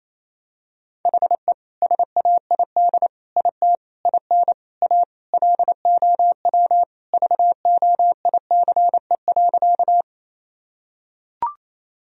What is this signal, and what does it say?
Morse code, heard as a keyed tone, about 28 wpm, 700 Hz. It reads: HE SAID IT IN A LOW VOICE.
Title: Sentences from Top 1000 Words - Review Part 1 - 28wpm